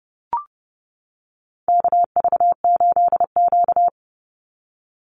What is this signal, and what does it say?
Morse code, heard as a keyed tone, about 30 wpm, 700 Hz. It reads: KV8Q
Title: Call Signs - 30wpm